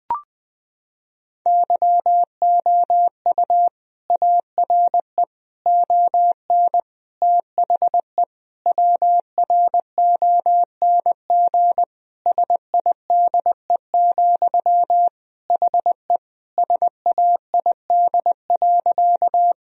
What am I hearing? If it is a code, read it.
Morse code, heard as a keyed tone, about 20 wpm, 700 Hz. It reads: YOU ARE ON THE WRONG SIDE, HE SAID.
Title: Sentences from Top 1000 Words - Review Part 1 - 20wpm